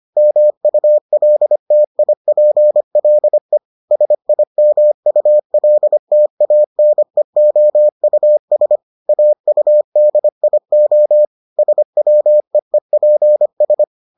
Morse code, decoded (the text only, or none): MULTIPLE SIMULTANEOUS AUDIO SWEEPS